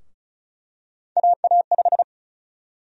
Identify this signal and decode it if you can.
Morse code, heard as a keyed tone, about 36 wpm, 700 Hz. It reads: AA5
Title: USA Call Sign Prefixes - Repeated 2x - 35wpm